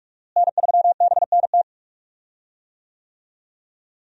Morse code, heard as a keyed tone, about 46 wpm, 700 Hz. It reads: N3BNT
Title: USA Call Signs - 5 Characters - Repeated 2x - 45wpm